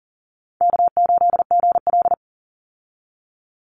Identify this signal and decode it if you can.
Morse code, heard as a keyed tone, about 40 wpm, 700 Hz. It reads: K8GL
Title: Call Signs - 40wpm